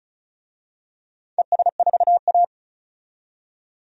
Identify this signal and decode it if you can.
Morse code, heard as a keyed tone, about 36 wpm, 700 Hz. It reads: ES4A
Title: DXCC-100 Call Signs - 4 Characters - 35wpm